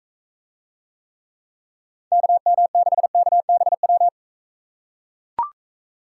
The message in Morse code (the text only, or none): KM6KBW